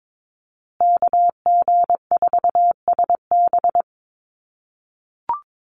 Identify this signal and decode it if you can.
Morse code, heard as a keyed tone, about 22 wpm, 700 Hz. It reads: KG4SB
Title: USA Call Signs Repeated 3x 22wpm